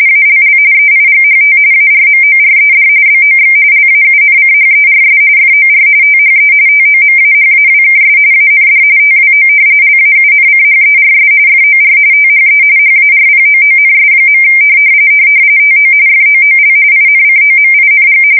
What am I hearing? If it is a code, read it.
RTTY, about 45.45 baud. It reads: ALL EIGHT ARMORED BOX LAUNCHERS FOR THE TOMAHAWKS AND TWO LAUNCHERS FOR THE HARPOONS. TO FOOL INCOMING
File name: fsk170 / battleshipnj